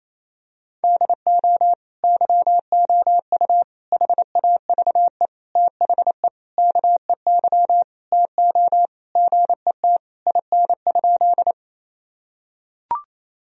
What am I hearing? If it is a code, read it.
Morse code, heard as a keyed tone, about 28 wpm, 700 Hz. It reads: DO YOU HAVE THE KEY TO GET IN?
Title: Sentences from Top 1000 Words - Review Part 4 - 28wpm